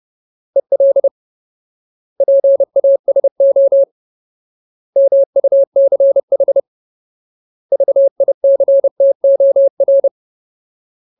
Morse code, heard as a keyed tone, about 30 wpm, 550 Hz. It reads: EL PASO MUCH VICTOR
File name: Ultimate Word List / 30WPM Ultimate